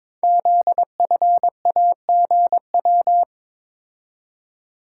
Morse code, heard as a keyed tone, about 22 wpm, 690 Hz. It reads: ZFAGW
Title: Code Groups - 5 Letters - 22wpm